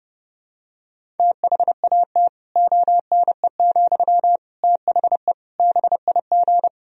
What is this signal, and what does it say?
Morse code, heard as a keyed tone, about 30 wpm, 700 Hz. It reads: THAT ONE, THE BIG
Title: Sentences from Top 1000 Words - Review Part 1 - 30wpm